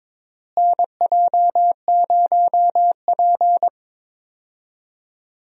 Morse code, heard as a keyed tone, about 22 wpm, 700 Hz. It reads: NJ0P
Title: USA Call Signs - 4 Characters - Repeated 3x - 22wpm